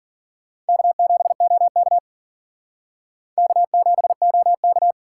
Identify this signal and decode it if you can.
Morse code, heard as a keyed tone, about 52 wpm, 700 Hz. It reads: K7OK K7OK